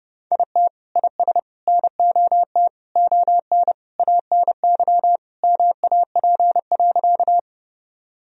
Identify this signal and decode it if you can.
Morse code, heard as a keyed tone, about 32 wpm, 700 Hz. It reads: IT IS NOT ON ANY MAP.
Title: Sentences from Top 1000 Words - Review Part 2 - 30wpm